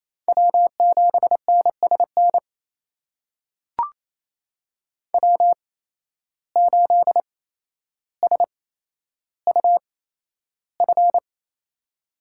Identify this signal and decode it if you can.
Morse code, heard as a keyed tone, about 28 wpm, spaced as for 12 wpm, 700 Hz. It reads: W7NSN W8SUF
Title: USA Call Signs - 5 Characters - 8x Character Spacing - 28wpm